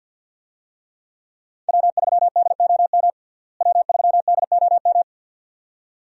Morse code, heard as a keyed tone, about 52 wpm, 700 Hz. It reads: W3DOM W3DOM
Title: USA Call Signs - 5 Characters - Repeated 2x - 50wpm